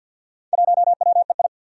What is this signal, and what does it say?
Morse code, heard as a keyed tone, about 53 wpm, 700 Hz. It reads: 1WEI
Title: Four Challenging Letter-Number Combinations 50wpm